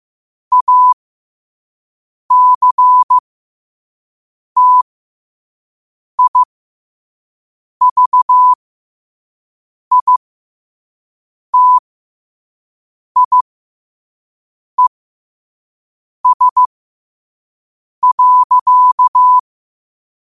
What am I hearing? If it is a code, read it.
Morse code, heard as a keyed tone, about 15 wpm, spaced as for 5 wpm, 1000 Hz. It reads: ACTIVITIES.
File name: qotdfiles / 5wpm